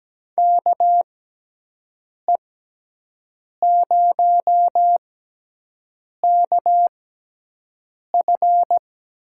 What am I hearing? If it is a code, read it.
Morse code, heard as a keyed tone, about 17 wpm, spaced as for 6 wpm, 700 Hz. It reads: KE0KF
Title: USA Call Signs - 6 Characters - 6x Character Spacing - 17wpm